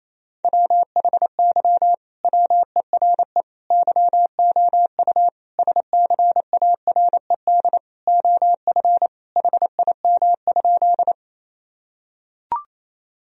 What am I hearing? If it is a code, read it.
Morse code, heard as a keyed tone, about 28 wpm, 700 Hz. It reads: WHY WERE YOU SCARED OF HIM?
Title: Sentences from Top 1000 Words - Review Part 4 - 28wpm